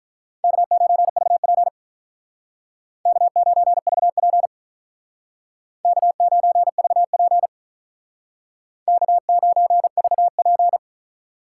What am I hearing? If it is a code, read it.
Morse code, heard as a keyed tone, about 46 wpm, 700 Hz. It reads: K9VP K9VP K9VP K9VP